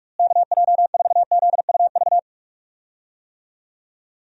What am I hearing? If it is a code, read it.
Morse code, heard as a keyed tone, about 46 wpm, 700 Hz. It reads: KJ4ZUV
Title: USA Call Signs - 6 Characters - Repeated 2x - 45wpm